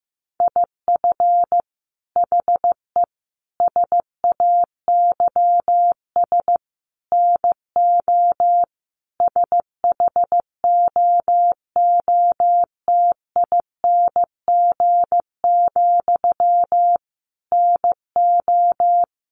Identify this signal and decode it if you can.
Morse code, heard as a keyed tone, about 15 wpm, 700 Hz. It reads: IF HE SAYS NO SHOOTING, NO
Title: A Sound Of Thunder 15wpm